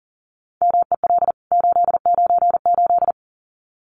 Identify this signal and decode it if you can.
Morse code, heard as a keyed tone, about 40 wpm, 700 Hz. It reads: MEL 898
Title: CWT Exchange 40wpm